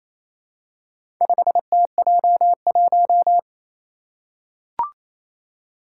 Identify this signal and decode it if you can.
Morse code, heard as a keyed tone, about 27 wpm, 700 Hz. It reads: 5TJ1
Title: Four Challenging Letter-Number Combinations 28wpm